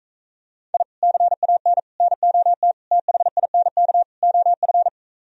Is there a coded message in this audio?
Yes: Morse code from 0.7 to 4.9 s, heard as a keyed tone, about 45 wpm, 700 Hz. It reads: I CAN NOT THINK OF